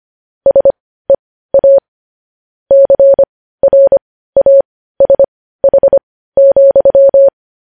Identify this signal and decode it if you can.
Morse code, heard as a keyed tone, about 25 wpm, spaced as for 15 wpm, 550 Hz. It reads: SEA CRASH,
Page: news headlines in morse code at 15 wpm